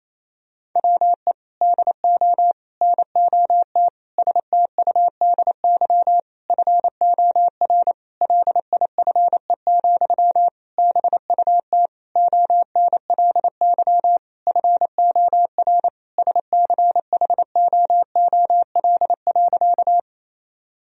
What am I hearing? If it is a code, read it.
Morse code, heard as a keyed tone, about 28 wpm, 700 Hz. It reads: WE DO NOT STUDY FOR LIFE, BUT ONLY FOR SCHOOL.
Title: Sentences from Top 1000 Words - Review Part 1 - 28wpm